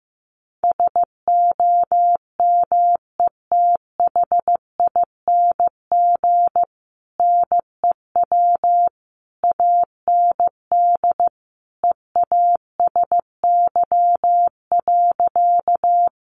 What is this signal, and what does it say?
Morse code, heard as a keyed tone, about 15 wpm, 700 Hz. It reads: SOMETHING NEW AND EASY.
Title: Sentences from Top 700 Words 15wpm